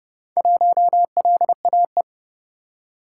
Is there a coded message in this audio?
Yes: Morse code from 0.4 to 2.0 s, heard as a keyed tone, about 29 wpm, 700 Hz. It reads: 1LAE